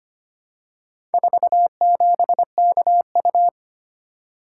Morse code, heard as a keyed tone, about 25 wpm, 690 Hz. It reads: 47KU
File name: practice / 25wpm / Four Letter-Number Combinations - 25wpm